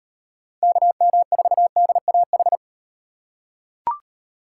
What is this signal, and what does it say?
Morse code, heard as a keyed tone, about 38 wpm, 700 Hz. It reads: KM4DAH